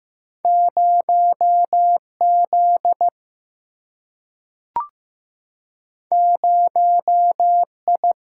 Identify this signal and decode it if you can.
Morse code, heard as a keyed tone, about 15 wpm, 700 Hz. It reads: 0Z 0I
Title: Single-Letter Single-Number 15wpm